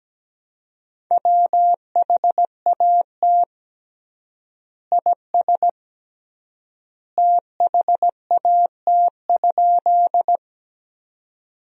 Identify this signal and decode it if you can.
Morse code, heard as a keyed tone, about 17 wpm, 700 Hz. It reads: WHAT IS THAT?
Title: 3x Word-Spacing - Sentences from Top 100 Words - 17wpm